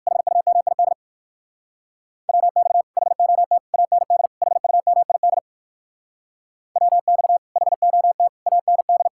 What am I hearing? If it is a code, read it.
Morse code, heard as a keyed tone, about 56 wpm, 700 Hz. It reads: HUMID WX HOT AND HUMID WX HOT AND